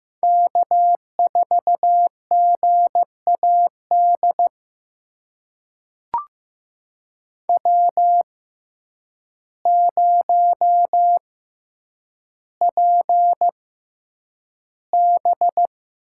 Morse code, heard as a keyed tone, about 15 wpm, spaced as for 8 wpm, 700 Hz. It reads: K4GAD W0PB
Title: USA Call Signs - 6x Character Spacing - 15wpm